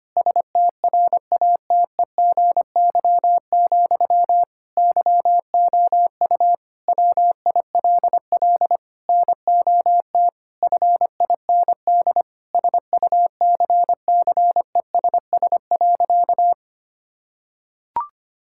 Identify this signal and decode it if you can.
Morse code, heard as a keyed tone, about 25 wpm, 700 Hz. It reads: STRATEGY, YOU WILL NOT FIND SUCCESS.
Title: Sentences from Top 1000 Words - Review Part 5 - 25wpm